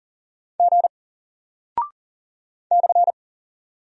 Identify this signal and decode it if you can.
Morse code, heard as a keyed tone, about 40 wpm, 700 Hz. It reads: G /
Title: Single Character (Letter:Number:Punctuation:Prosign) 40wpm